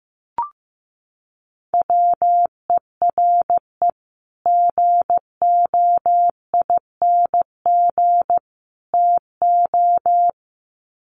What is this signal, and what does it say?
Morse code, heard as a keyed tone, about 15 wpm, 700 Hz. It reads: WERE GOING TO
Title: Sets of 3 Words 15wpm